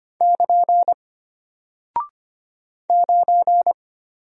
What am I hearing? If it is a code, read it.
Morse code, heard as a keyed tone, about 25 wpm, 700 Hz. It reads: <KN> 9
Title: Single Character (Letter:Number:Punctuation:Prosign) Alt 25wpm